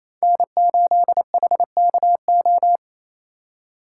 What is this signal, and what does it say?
Morse code, heard as a keyed tone, about 33 wpm, 700 Hz. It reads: N8HKO
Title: Speed Racing - USA 1x3 Call Signs 28wpm